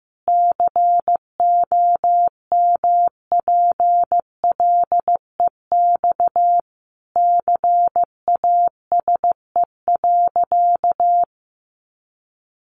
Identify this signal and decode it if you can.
Morse code, heard as a keyed tone, about 15 wpm, 700 Hz. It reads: COMPLEX CASE.
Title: Sentences from Top 900 Words 15wpm